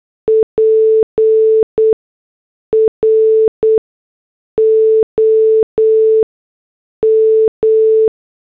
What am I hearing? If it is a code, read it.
Morse code, heard as a keyed tone, about 8 wpm, spaced as for 6 wpm, 440 Hz. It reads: PROM
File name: morse / headlines